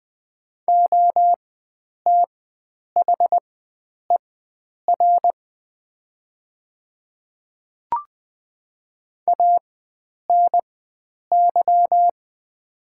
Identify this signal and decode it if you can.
Morse code, heard as a keyed tone, about 20 wpm, spaced as for 9 wpm, 700 Hz. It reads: OTHER ANY
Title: 4x Character Spacing - Top 300 Words - 20wpm